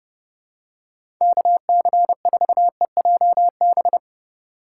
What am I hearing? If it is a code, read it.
Morse code, heard as a keyed tone, about 32 wpm, 700 Hz. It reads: KC4EJB
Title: USA Call Signs - 3x Character Spacing - 30wpm